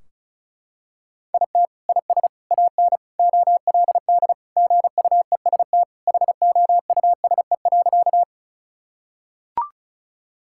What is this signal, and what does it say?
Morse code, heard as a keyed tone, about 34 wpm, 700 Hz. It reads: IT IS AN OLD GUEST HOUSE.
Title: Sentences from Top 1600 Words 35wpm